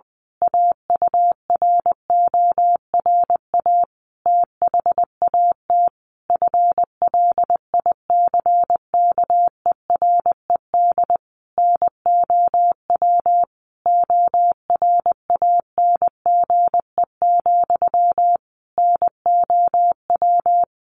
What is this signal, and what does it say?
Morse code, heard as a keyed tone, about 20 wpm, 700 Hz. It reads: AURORA THAT FLICKERED NOW ORANGE, NOW
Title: A Sound Of Thunder 20wpm